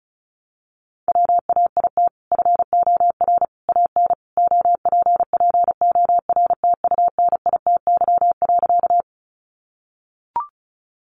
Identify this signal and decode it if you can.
Morse code, heard as a keyed tone, about 34 wpm, 700 Hz. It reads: WAIT FOR AN OPPORTUNITY.
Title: Sentences from Top 1000 Words 35wpm